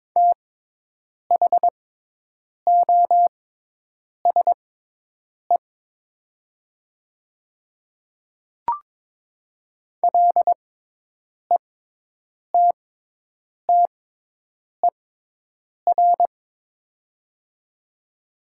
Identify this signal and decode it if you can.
Morse code, heard as a keyed tone, about 22 wpm, spaced as for 8 wpm, 700 Hz. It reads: THOSE LETTER
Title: 6x Character Spacing - Top 300 Words - 22wpm